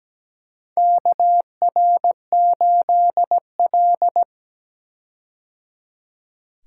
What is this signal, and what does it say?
Morse code, heard as a keyed tone, about 17 wpm, 700 Hz. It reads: KR8L
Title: K1USN SST QSO Practice - 17wpm